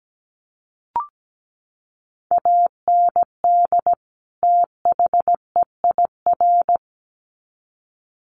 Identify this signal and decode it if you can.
Morse code, heard as a keyed tone, about 17 wpm, 700 Hz. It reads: AND THEIR
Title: Sets of 2 Words 17wpm